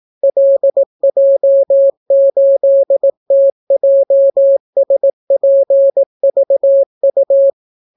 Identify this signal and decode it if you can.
Morse code, heard as a keyed tone, about 18 wpm, 550 Hz. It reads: LJ8TJSPVU